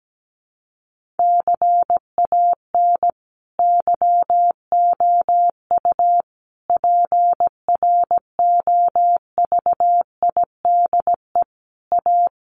Morse code, heard as a keyed tone, about 17 wpm, 700 Hz. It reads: CAN YOU PROVIDE A
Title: Sentences from Top 1500 Words 17wpm